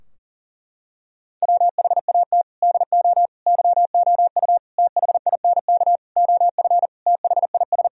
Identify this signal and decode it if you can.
Morse code, heard as a keyed tone, about 40 wpm, 700 Hz. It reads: WHAT DO YOU THINK OF THIS
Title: Sentences from Top 1000 Words - Review Part 2 - 40wpm